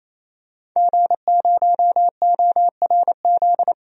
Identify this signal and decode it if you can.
Morse code, heard as a keyed tone, about 28 wpm, 700 Hz. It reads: G0ORZ
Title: DXCC-100 Call Signs - 5 Characters - 28wpm